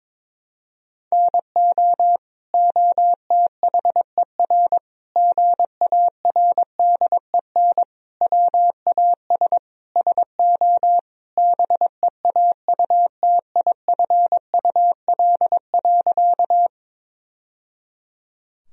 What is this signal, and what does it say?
Morse code, heard as a keyed tone, about 22 wpm, 700 Hz. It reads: NO OTHER GARDEN WAS SO BEAUTIFUL.
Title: Sentences from Top 1000 Words - Review Part 4 - 22wpm